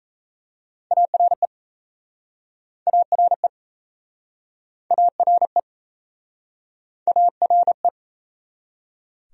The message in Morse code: ARE ARE ARE ARE

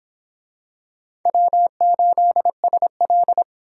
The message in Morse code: W8SL

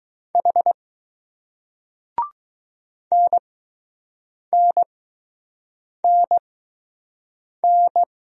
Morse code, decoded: H NNNN

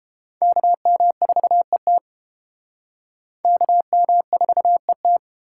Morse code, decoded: KM4ET KM4ET

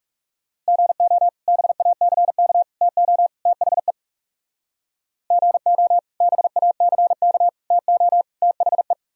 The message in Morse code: GO BACK TO THE GO BACK TO THE

